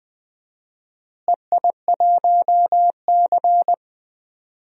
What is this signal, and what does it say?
Morse code, heard as a keyed tone, about 20 wpm, 700 Hz. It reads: EI1C
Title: DXCC-100 Call Signs - 4 Characters - Repeated 3x - 20wpm